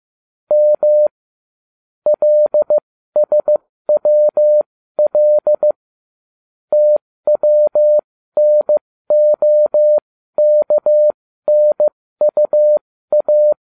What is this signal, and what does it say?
Morse code, heard as a keyed tone, about 15 wpm, spaced as for 13 wpm, 600 Hz. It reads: M LSWL TWNOKNUA